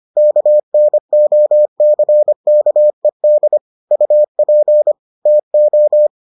Morse code, heard as a keyed tone, about 25 wpm, 600 Hz. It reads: KNOCKED UP TO